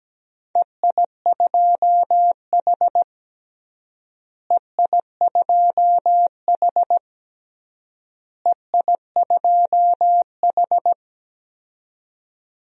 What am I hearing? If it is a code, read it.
Morse code, heard as a keyed tone, about 17 wpm, 700 Hz. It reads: EI2H EI2H EI2H
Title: DXCC-100 Call Signs - 4 Characters - Repeated 3x - 17wpm